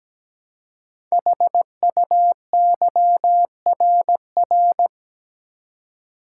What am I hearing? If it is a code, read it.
Morse code, heard as a keyed tone, about 17 wpm, 700 Hz. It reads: HUYRR
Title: Code Groups - 5 Letters - 17wpm